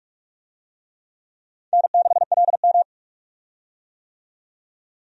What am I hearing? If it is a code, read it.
Morse code, heard as a keyed tone, about 44 wpm, 700 Hz. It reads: N6LM
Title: USA Call Signs - 4 Characters - 45wpm